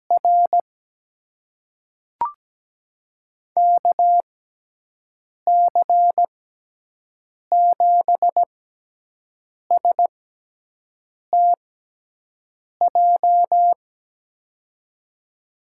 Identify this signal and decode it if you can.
Morse code, heard as a keyed tone, about 17 wpm, spaced as for 6 wpm, 700 Hz. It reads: R KC7STJ
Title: USA Call Signs - 6x Character Spacing - 17wpm